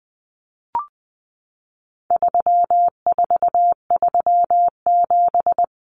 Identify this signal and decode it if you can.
Morse code, heard as a keyed tone, about 20 wpm, 700 Hz. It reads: 3437
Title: Common Cut Numbers - 4-Digit 20wpm